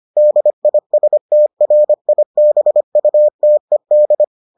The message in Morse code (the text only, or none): DISTRIBUTED